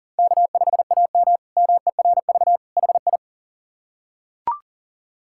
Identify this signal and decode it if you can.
Morse code, heard as a keyed tone, about 40 wpm, 700 Hz. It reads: K5AM MERV HI
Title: K1USN SST QSO Practice - 40wpm